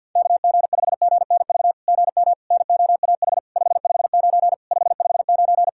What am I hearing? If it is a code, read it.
Morse code, heard as a keyed tone, about 50 wpm, 700 Hz. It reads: KG5GNV GM NOAH 559 559
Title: SOTA QSO Practice 50wpm